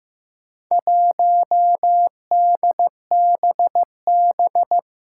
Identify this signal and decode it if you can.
Morse code, heard as a keyed tone, about 15 wpm, 700 Hz. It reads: 1DBB
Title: Four Challenging Letter-Number Combinations 15wpm